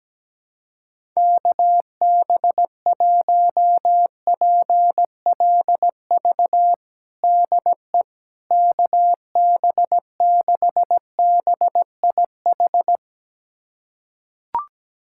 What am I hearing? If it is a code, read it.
Morse code, heard as a keyed tone, about 17 wpm, 700 Hz. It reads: KB1PLV DE KB6BIH
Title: USA Callsign DE Callsign - 6 Characters - 17wpm